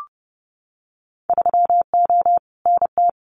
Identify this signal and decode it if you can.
Morse code, heard as a keyed tone, about 30 wpm, 700 Hz. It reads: 3O NT
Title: Winter Field Day Exchanges Encore 30wpm